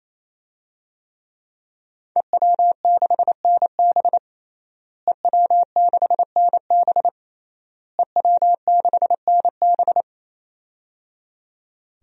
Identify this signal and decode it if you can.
Morse code, heard as a keyed tone, about 28 wpm, 700 Hz. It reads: EW6NB EW6NB EW6NB
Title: DXCC-100 Call Signs - 5 Characters - Repeated 3x - 28wpm